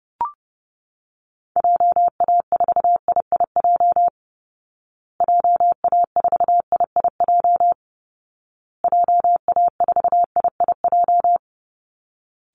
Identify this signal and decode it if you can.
Morse code, heard as a keyed tone, about 30 wpm, 700 Hz. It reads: JA4IIJ JA4IIJ JA4IIJ
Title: Call Signs Repeated 2x - 30wpm